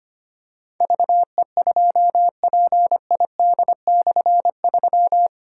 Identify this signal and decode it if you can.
Morse code, heard as a keyed tone, about 25 wpm, 700 Hz. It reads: VE2PID/3
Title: K1USN SST QSO Practice - 25wpm